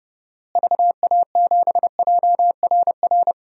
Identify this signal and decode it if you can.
Morse code, heard as a keyed tone, about 29 wpm, 700 Hz. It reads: VA7JRR
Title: DXCC-100 Call Signs - 6 Characters - Repeated 3x - 30wpm